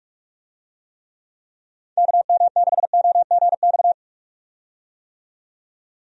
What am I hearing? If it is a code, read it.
Morse code, heard as a keyed tone, about 46 wpm, 700 Hz. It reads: KM6OGX
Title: USA Call Signs - 6 Characters - 45wpm